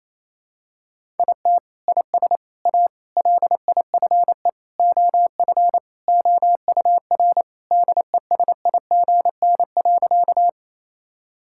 Morse code, heard as a keyed tone, about 28 wpm, 700 Hz. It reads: IT IS A LIFE OF OUR DESIGN.
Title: Sentences from Top 1000 Words - Review Part 5 - 28wpm